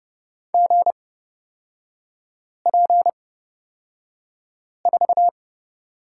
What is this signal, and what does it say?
Morse code, heard as a keyed tone, about 30 wpm, spaced as for 5 wpm, 700 Hz. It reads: GP4
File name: Lesson 36e - BT - Three Characters 30wpm